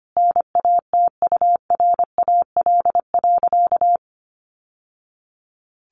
Morse code, heard as a keyed tone, about 25 wpm, 700 Hz. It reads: NATURAL.
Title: Sentences from Top 700 Words 25wpm